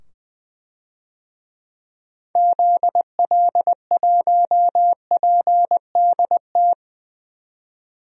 Morse code, heard as a keyed tone, about 20 wpm, 700 Hz. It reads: ZL1PDT
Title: DXCC-100 Call Signs - 6 Characters - 20wpm